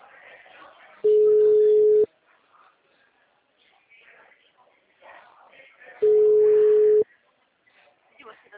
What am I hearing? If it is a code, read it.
Ringback tone (Europe, 425 Hz), with speech.
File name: out- -2017- - - .6159